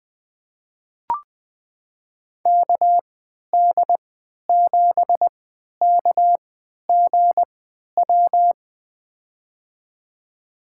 Morse code, heard as a keyed tone, about 20 wpm, spaced as for 11 wpm, 700 Hz. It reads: KD7KGW
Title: USA Call Signs - 6 Characters - 3x Character Spacing - 20wpm